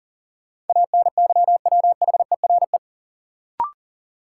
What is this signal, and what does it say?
Morse code, heard as a keyed tone, about 40 wpm, 700 Hz. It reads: ANYWHERE
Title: Sentences from Top 1400 Words - New Word Drill - 40wpm